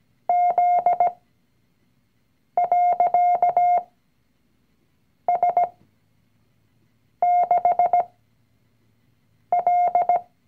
Morse code, heard as a keyed tone, about 17 wpm, spaced as for 5 wpm, 710 Hz. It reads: Z.S6L
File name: CW-Online-Kurs / Anfänger / Lektion 019